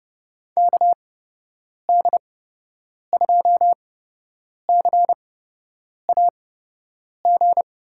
Morse code, heard as a keyed tone, about 30 wpm, spaced as for 8 wpm, 700 Hz. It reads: KD2CAG
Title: USA Call Signs - 6 Characters - 8x Character Spacing - 30wpm